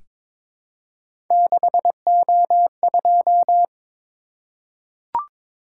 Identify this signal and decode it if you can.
Morse code, heard as a keyed tone, about 22 wpm, 700 Hz. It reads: 6O2
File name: practice / 22wpm / Three Letter-Number Combinations - 22wpm